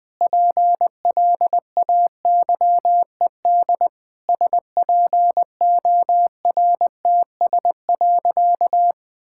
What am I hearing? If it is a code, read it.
Morse code, heard as a keyed tone, about 20 wpm, 700 Hz. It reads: PLAYED SPORTS.
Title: Sentences with 6-Word Phrases - 20wpm